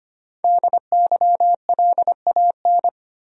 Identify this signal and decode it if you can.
Morse code, heard as a keyed tone, about 25 wpm, 700 Hz. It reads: DYLAN
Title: USA Common Names - Max 5 Letters - 25wpm